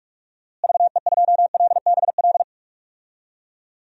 Morse code, heard as a keyed tone, about 41 wpm, 700 Hz. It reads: VE2LBL
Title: DXCC-100 Call Signs - 6 Characters - Repeated 3x - 45wpm